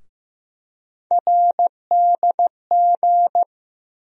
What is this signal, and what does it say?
Morse code, heard as a keyed tone, about 15 wpm, 700 Hz. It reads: RDG
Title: USA Call Sign Postfixes - 3 Letters 15wpm